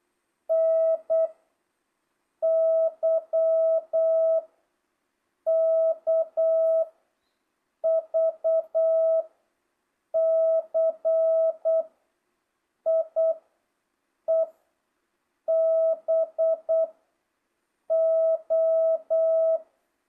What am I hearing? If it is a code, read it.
Morse code, heard as a keyed tone, about 8 wpm, 660 Hz. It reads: N Y K V C I E B O
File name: jh8cbh / 2025cw